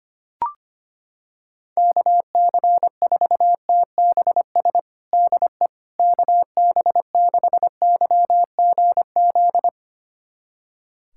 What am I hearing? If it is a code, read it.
Morse code, heard as a keyed tone, about 25 wpm, 700 Hz. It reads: KC4TBS DE KB6YGZ
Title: USA Callsign DE Callsign - 6 Characters - 25wpm